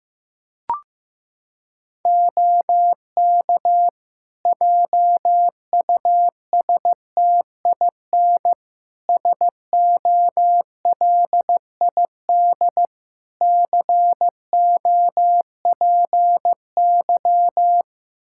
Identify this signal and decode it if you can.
Morse code, heard as a keyed tone, about 15 wpm, 700 Hz. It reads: OK JUSTIN SOLID COPY
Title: Practice QSO - Exchange 3 - More Info - 15wpm